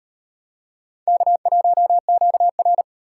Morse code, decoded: K1QR

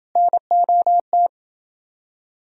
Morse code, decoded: NOT

